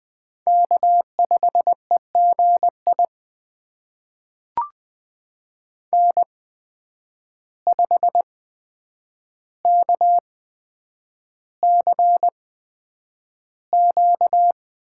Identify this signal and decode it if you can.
Morse code, heard as a keyed tone, about 20 wpm, spaced as for 9 wpm, 700 Hz. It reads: K5EGI N5KCQ
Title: USA Call Signs - 5 Characters - 8x Character Spacing - 20wpm